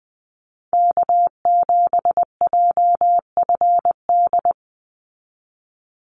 Morse code, heard as a keyed tone, about 20 wpm, 700 Hz. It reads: K7JFD
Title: NA SOTA Chaser Callsigns 20wpm